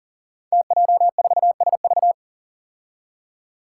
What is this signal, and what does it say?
Morse code, heard as a keyed tone, about 39 wpm, 700 Hz. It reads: TJ4SV
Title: Five Challenging Letter-Number Combinations 40wpm